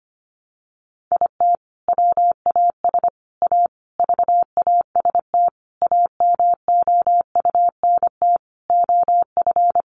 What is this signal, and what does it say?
Morse code, heard as a keyed tone, about 25 wpm, 700 Hz. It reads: IT WAS A VAST AMOUNT OF
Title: Sentences from Top 1300 Words 25wpm